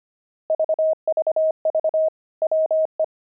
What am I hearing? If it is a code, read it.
Morse code, heard as a keyed tone, about 25 wpm, 630 Hz. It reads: VVV WE
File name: pangram-fs-25-cs-25-buckles